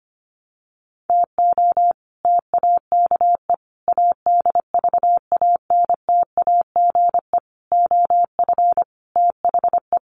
Morse code, heard as a keyed tone, about 25 wpm, 700 Hz. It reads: TO TAKE ADVANTAGE OF THE
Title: Sets of 5 Words 25wpm